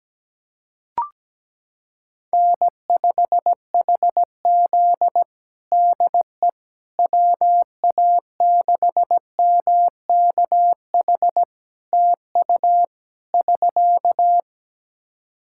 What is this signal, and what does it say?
Morse code, heard as a keyed tone, about 17 wpm, 700 Hz. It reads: N5HZ DE WA6MKH TU <SK>
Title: Practice QSO - Exchange 4 - Ending - 17wpm